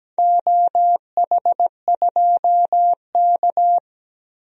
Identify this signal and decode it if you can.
Morse code, heard as a keyed tone, about 17 wpm, 700 Hz. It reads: OH2K
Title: DXCC-100 Call Signs - 4 Characters - Repeated 2x - 17wpm